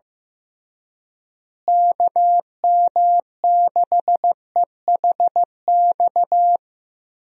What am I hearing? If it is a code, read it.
Morse code, heard as a keyed tone, about 15 wpm, 700 Hz. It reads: KM6EHX
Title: USA Call Signs - 6 Characters - Repeated 2x - 15wpm